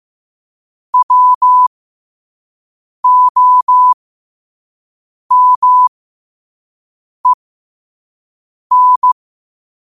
Morse code, heard as a keyed tone, about 15 wpm, spaced as for 5 wpm, 1000 Hz. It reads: WOMEN
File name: qotdfiles / 5wpm